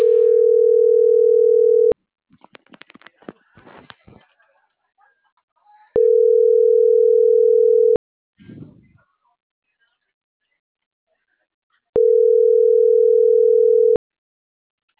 Ringback tone (North America), 440 + 480 Hz.